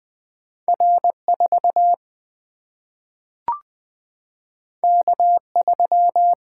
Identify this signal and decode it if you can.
Morse code, heard as a keyed tone, about 20 wpm, 700 Hz. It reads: R 4 K 3
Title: Single-Letter Single-Number 20wpm